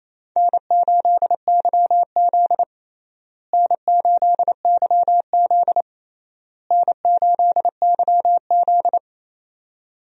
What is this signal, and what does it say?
Morse code, heard as a keyed tone, about 28 wpm, 700 Hz. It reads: N8YZ N8YZ N8YZ
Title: USA Call Signs - 4 Characters - Repeated 3x - 28wpm